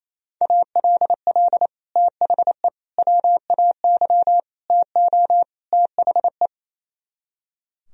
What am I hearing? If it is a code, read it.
Morse code, heard as a keyed tone, about 34 wpm, 700 Hz. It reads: ALL THE WAY TO THE
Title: Speed-Racing - Sets of 5 Words 28wpm